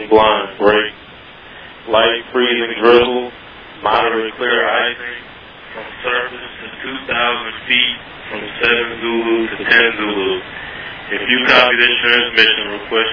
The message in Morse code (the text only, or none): none